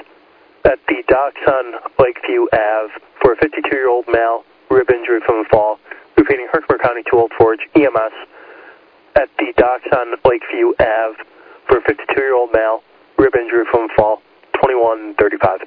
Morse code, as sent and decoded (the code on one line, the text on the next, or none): none